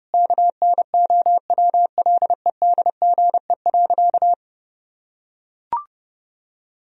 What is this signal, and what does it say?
Morse code, heard as a keyed tone, about 30 wpm, 700 Hz. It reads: KNOWLEDGE.
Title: Sentences from Top 1000 Words - Review Part 4 - 30wpm